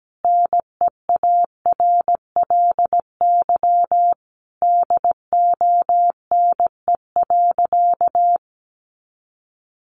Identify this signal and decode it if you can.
Morse code, heard as a keyed tone, about 17 wpm, 700 Hz. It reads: NEARLY DONE.
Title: Sentences from Top 1300 Words 17wpm